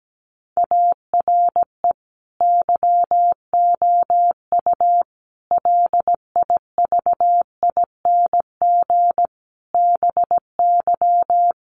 Morse code, heard as a keyed tone, about 17 wpm, 700 Hz. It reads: ARE YOU LIVING BY
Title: Sentences from Top 1100 Words 17wpm